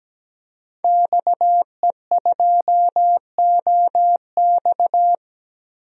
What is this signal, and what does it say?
Morse code, heard as a keyed tone, about 17 wpm, 700 Hz. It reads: XE2OX
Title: DXCC-100 Call Signs - 5 Characters - 17wpm